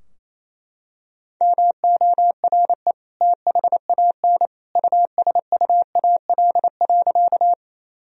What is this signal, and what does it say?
Morse code, heard as a keyed tone, about 28 wpm, 700 Hz. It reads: MORE THAN USUAL.
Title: Sentences from Top 1000 Words - Review Part 2 - 28wpm